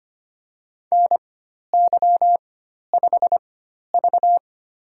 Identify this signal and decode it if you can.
Morse code, heard as a keyed tone, about 25 wpm, spaced as for 12 wpm, 700 Hz. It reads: NY5V
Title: USA Call Signs - 4 Characters - 4x Character Spacing - 25wpm